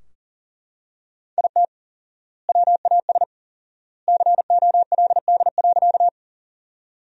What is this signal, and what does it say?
Morse code, heard as a keyed tone, about 40 wpm, 700 Hz. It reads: IT WAS COLD.
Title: 4x Word-Spacing - Sentences from Top 400 Words - 40wpm